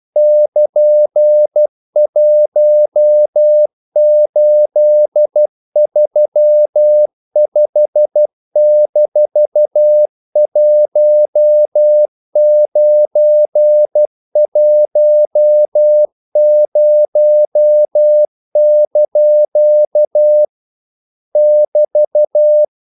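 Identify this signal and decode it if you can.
Morse code, heard as a keyed tone, about 12 wpm, 600 Hz. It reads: <KN>1835-1910) <BT>
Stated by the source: Morse practice sending English quotations.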